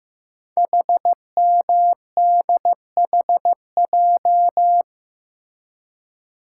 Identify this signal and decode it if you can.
Morse code, heard as a keyed tone, about 15 wpm, 700 Hz. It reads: HMDHJ
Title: Code Groups - 5 Letters - 15wpm